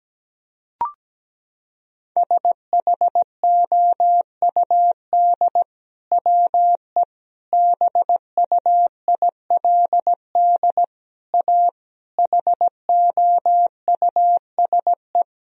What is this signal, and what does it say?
Morse code, heard as a keyed tone, about 17 wpm, 700 Hz. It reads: SHOUD WE BUILD A HOUSE
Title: Sentences from Top 1000 Words - Review Part 4 - 17wpm